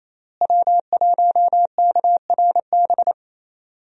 Morse code, heard as a keyed tone, about 28 wpm, 700 Hz. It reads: W1KRB
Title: USA Call Signs - 5 Characters - Repeated 2x - 28wpm